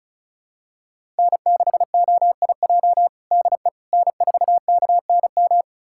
Morse code, heard as a keyed tone, about 36 wpm, 700 Hz. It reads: N6OIJ DE N4KNM
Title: USA Callsign DE Callsign - 5 Characters - 35wpm